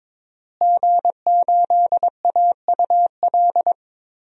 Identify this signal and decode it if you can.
Morse code, heard as a keyed tone, about 22 wpm, 700 Hz. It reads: G8AUL
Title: DXCC-100 Call Signs - 5 Characters - Repeated 2x - 22wpm